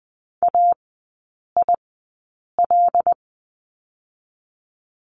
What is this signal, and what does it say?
Morse code, heard as a keyed tone, about 20 wpm, spaced as for 8 wpm, 700 Hz. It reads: AIL
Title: 093 Lesson 16e - Letter c - Three Characters 20wpm